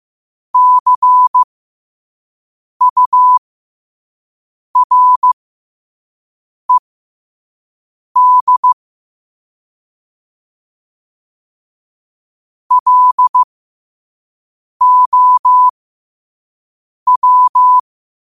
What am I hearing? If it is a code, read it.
Morse code, heard as a keyed tone, about 15 wpm, spaced as for 5 wpm, 1000 Hz. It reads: CURED LOW
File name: qotdfiles / 5wpm